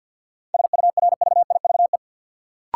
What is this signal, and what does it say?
Morse code, heard as a keyed tone, about 51 wpm, 700 Hz. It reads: SURVIVE
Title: Sentences from Top 1300 Words - New Word Drill - 50wpm